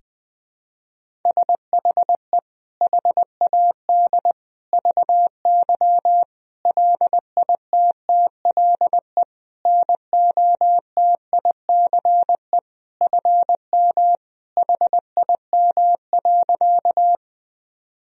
Morse code, heard as a keyed tone, about 21 wpm, 700 Hz. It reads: SHE HAD VY LITTLE NOTICE FM HIM.
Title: Sentences from Top 400 Words and Abbreviations 20wpm